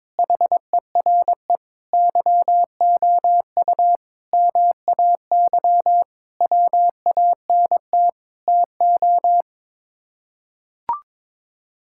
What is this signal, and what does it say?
Morse code, heard as a keyed tone, about 22 wpm, 700 Hz. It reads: HERE YOU MAY WANT TO
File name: practice / 22wpm / Sets of 8 Words - 22wpm